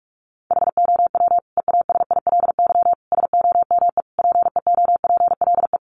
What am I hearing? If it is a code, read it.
Morse code, heard as a keyed tone, about 44 wpm, 700 Hz. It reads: HOW EASILY SOME PEOPLE
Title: Sentences from Top 900 Words 45wpm